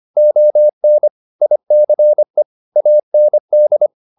Morse code, heard as a keyed tone, about 25 wpm, 600 Hz. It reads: ON ICE AND
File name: the old man and the sea part 1 25wpm